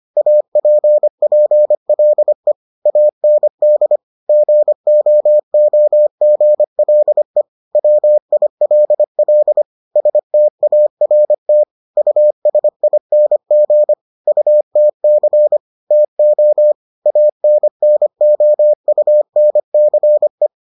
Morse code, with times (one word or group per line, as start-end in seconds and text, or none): 0.2-2.5 APPLE
2.9-4.0 AND
4.3-7.4 GOOGLE
7.8-9.6 WILL
10.0-11.6 START
12.0-13.9 USING
14.3-15.6 UTC
15.9-16.7 TO
17.1-20.5 ANNOUNCE